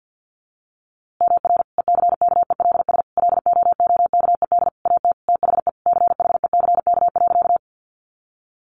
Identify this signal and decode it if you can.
Morse code, heard as a keyed tone, about 50 wpm, 700 Hz. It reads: MR ECKELS LOOKED AT THE CHECK.
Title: A Sound Of Thunder 50wpm